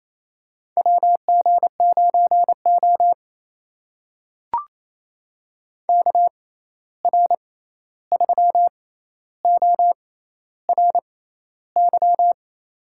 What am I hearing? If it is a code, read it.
Morse code, heard as a keyed tone, about 28 wpm, spaced as for 13 wpm, 700 Hz. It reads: WG9O KR3ORY
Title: USA Call Signs - 6x Character Spacing - 28wpm